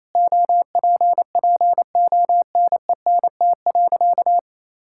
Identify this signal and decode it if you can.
Morse code, heard as a keyed tone, about 27 wpm, 700 Hz. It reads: OPPONENT.
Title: Pangram Challenge v2 - 28wpm